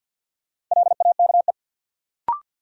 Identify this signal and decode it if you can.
Morse code, heard as a keyed tone, about 46 wpm, 700 Hz. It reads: LAKE
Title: Sentences from Top 1100 Words - New Word Drill Alt - 50wpm